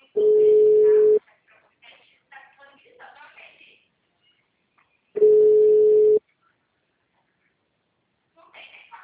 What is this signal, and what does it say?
Ringback tone (Europe, 425 Hz), with speech.
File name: out- -2007- - - .2656